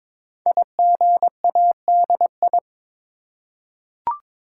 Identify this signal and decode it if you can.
Morse code, heard as a keyed tone, about 22 wpm, 700 Hz. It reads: IGADI
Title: Code Groups - 5 Letters - 22wpm